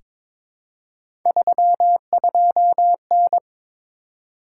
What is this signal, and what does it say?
Morse code, heard as a keyed tone, about 22 wpm, 700 Hz. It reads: 32N
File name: practice / 22wpm / Three Letter-Number Combinations - 22wpm